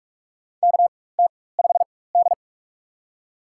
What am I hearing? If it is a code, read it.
Morse code, heard as a keyed tone, about 43 wpm, spaced as for 21 wpm, 700 Hz. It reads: KT5D
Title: USA Call Signs - 4 Characters - 4x Character Spacing - 45wpm